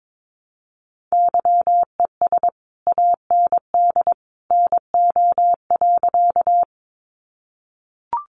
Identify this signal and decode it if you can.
Morse code, heard as a keyed tone, about 22 wpm, 700 Hz. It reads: YES AND NO.
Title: Sentences from Top 500 Words 22wpm